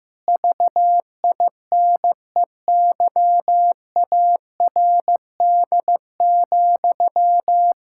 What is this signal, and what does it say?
Morse code, heard as a keyed tone, about 15 wpm, 700 Hz. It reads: VINEYARD,
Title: Pangram Challenge v2 - 15wpm